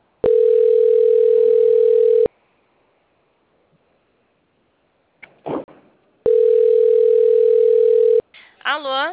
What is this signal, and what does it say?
Ringback tone (North America), 440 + 480 Hz, with speech.
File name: out- - - - - .19369